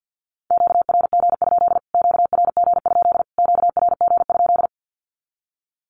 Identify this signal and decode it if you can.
Morse code, heard as a keyed tone, about 50 wpm, 700 Hz. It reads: QRG? QRG? QRG?
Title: Q-Codes Repeated 3x 50wpm